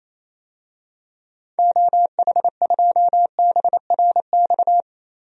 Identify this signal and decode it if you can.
Morse code, heard as a keyed tone, about 28 wpm, 700 Hz. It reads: OH2BRX
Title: DXCC-100 Call Signs - 6 Characters - 28wpm